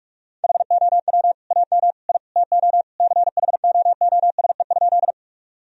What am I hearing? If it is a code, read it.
Morse code, heard as a keyed tone, about 46 wpm, 700 Hz. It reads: HOW AM I TO CHOOSE?
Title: Sentences from Top 1000 Words - Review Part 4 - 45wpm